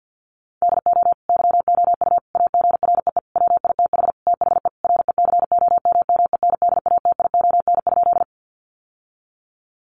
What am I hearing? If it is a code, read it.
Morse code, heard as a keyed tone, about 50 wpm, 700 Hz. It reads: DO YOU AGREE WITH THE RECOMMENDATION?
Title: Sentences from Top 1400 Words 50wpm